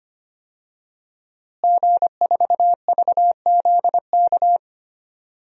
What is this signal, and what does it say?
Morse code, heard as a keyed tone, about 25 wpm, 700 Hz. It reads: G4VZK
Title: DXCC-100 Call Signs - 5 Characters - Repeated 2x - 25wpm